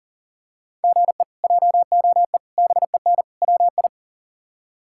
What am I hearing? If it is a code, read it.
Morse code, heard as a keyed tone, about 40 wpm, 700 Hz. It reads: GE JOE BEN WI
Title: K1USN SST QSO Practice - 40wpm